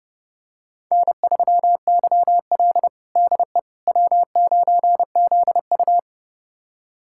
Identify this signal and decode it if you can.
Morse code, heard as a keyed tone, about 32 wpm, 700 Hz. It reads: N3YL DE W9ZU
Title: USA Callsign DE Callsign - 4 Characters - 30wpm